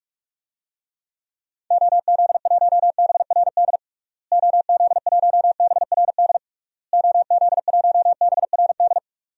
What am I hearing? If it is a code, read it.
Morse code, heard as a keyed tone, about 46 wpm, 700 Hz. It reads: OZ1BRD OZ1BRD OZ1BRD
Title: DXCC-100 Call Signs - 6 Characters - Repeated 3x - 45wpm